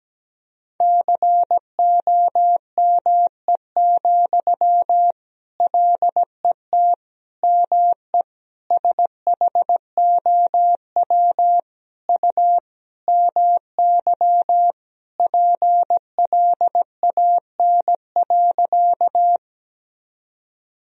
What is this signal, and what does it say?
Morse code, heard as a keyed tone, about 17 wpm, 700 Hz. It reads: COME, LET ME SHOW U MY PLAN.
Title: Sentences from Top 500 Words and Abbreviations 17wpm